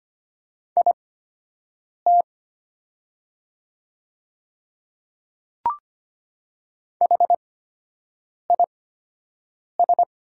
Morse code, heard as a keyed tone, about 25 wpm, spaced as for 7 wpm, 700 Hz. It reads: IT HIS
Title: 8x Character Spacing - Top 100 Words - 25wpm